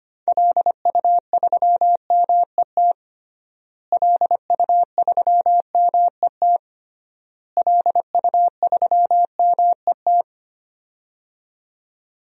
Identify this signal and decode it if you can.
Morse code, heard as a keyed tone, about 25 wpm, 700 Hz. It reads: LU3MET LU3MET LU3MET
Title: DXCC-100 Call Signs - 6 Characters - Repeated 3x - 25wpm